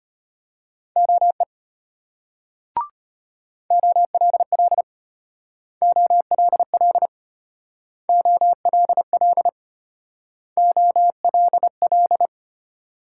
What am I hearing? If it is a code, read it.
Morse code, heard as a keyed tone, about 32 wpm, 700 Hz. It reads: OE OLL OLL OLL OLL